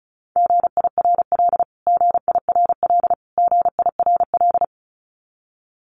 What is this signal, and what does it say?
Morse code, heard as a keyed tone, about 34 wpm, 700 Hz. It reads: GIRL GIRL GIRL
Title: Top 500 Words Repeated 35wpm